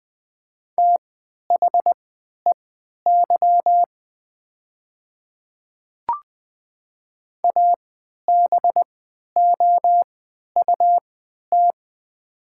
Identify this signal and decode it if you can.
Morse code, heard as a keyed tone, about 20 wpm, spaced as for 11 wpm, 700 Hz. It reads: THEY ABOUT
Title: 3x Character Spacing - Top 100 Words - 20wpm